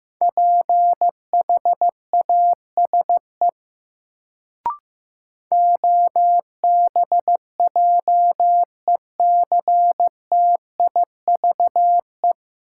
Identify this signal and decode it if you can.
Morse code, heard as a keyed tone, about 15 wpm, 700 Hz. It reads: PHASE OBJECTIVE
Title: Sentences from Top 1200 Words - New Word Drill - 15wpm